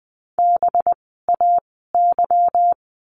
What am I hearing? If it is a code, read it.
Morse code, heard as a keyed tone, about 20 wpm, 700 Hz. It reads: B A Y
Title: 185 Lesson 30f - Letter b - Words 20wpm